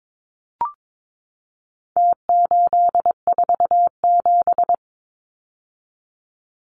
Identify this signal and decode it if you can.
Morse code, heard as a keyed tone, about 22 wpm, 700 Hz. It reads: T847
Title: Numbers - Contest Style - 4-Digit 22wpm